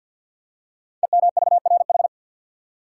Morse code, heard as a keyed tone, about 50 wpm, 700 Hz. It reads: EMVRH
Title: Code Groups - 5 Letters - 50wpm